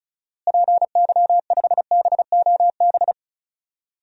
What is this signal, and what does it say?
Morse code, heard as a keyed tone, about 34 wpm, 700 Hz. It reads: PY5BOB
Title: DXCC-100 Call Signs - 6 Characters - Repeated 2x - 35wpm